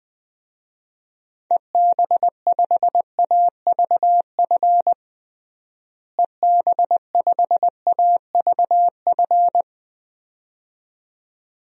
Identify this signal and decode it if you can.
Morse code, heard as a keyed tone, about 21 wpm, 700 Hz. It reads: EB5AVF EB5AVF
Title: DXCC-100 Call Signs - 6 Characters - Repeated 2x - 20wpm